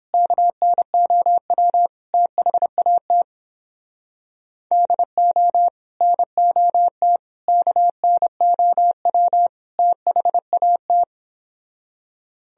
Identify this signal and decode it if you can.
Morse code, heard as a keyed tone, about 30 wpm, 700 Hz. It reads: KNOW THAT DO NOT KNOW THAT